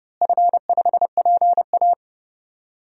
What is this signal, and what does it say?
Morse code, heard as a keyed tone, about 32 wpm, 700 Hz. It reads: F5PA
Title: DXCC-100 Call Signs - 4 Characters - Repeated 2x - 30wpm